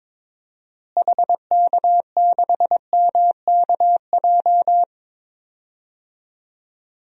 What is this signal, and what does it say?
Morse code, heard as a keyed tone, about 21 wpm, 700 Hz. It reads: HK6MKJ
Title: DXCC-100 Call Signs - 6 Characters - Repeated 3x - 22wpm